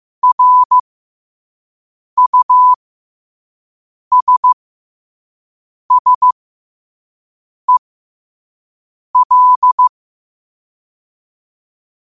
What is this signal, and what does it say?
Morse code, heard as a keyed tone, about 15 wpm, spaced as for 5 wpm, 1000 Hz. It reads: RUSSEL
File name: qotdfiles / 5wpm